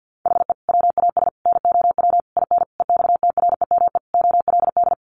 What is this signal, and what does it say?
Morse code, heard as a keyed tone, about 50 wpm, 700 Hz. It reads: HE WAS NOW IN EXTREME OLD
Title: Sentences from Top 1300 Words 50wpm